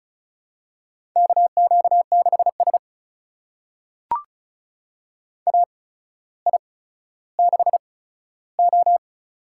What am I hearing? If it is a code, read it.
Morse code, heard as a keyed tone, about 35 wpm, spaced as for 15 wpm, 700 Hz. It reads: KQ6S AI6O
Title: USA Call Signs - 4 Characters - 8x Character Spacing - 35wpm